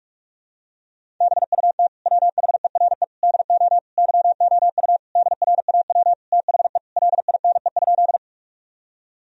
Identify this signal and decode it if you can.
Morse code, heard as a keyed tone, about 41 wpm, 700 Hz. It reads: BUT WHERE DO YOU DRAW THE LINE?
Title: Sentences from Top 1000 Words - Review Part 1 - 45wpm